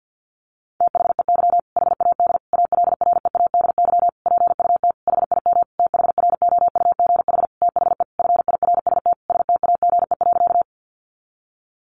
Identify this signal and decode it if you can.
Morse code, heard as a keyed tone, about 50 wpm, 700 Hz. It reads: THEY HAD ALREADY PUT HIM THROUGH THE FIRST STAGE.
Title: Sentences from Top 1200 Words 50wpm